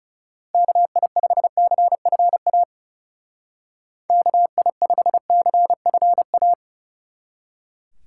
Morse code, heard as a keyed tone, about 37 wpm, 700 Hz. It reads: KI5CFA KI5CFA
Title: Speed Racing - USA 2x3 Call Signs 30wpm